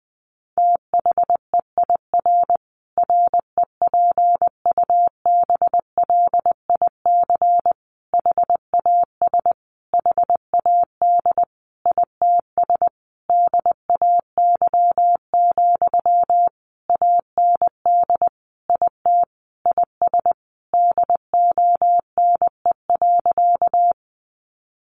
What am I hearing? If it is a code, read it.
Morse code, heard as a keyed tone, about 20 wpm, 700 Hz. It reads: THEIR REPUBLIC HAS HAD ITS DAY, AND IT IS DONE.
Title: Sentences from Top 1500 Words 20wpm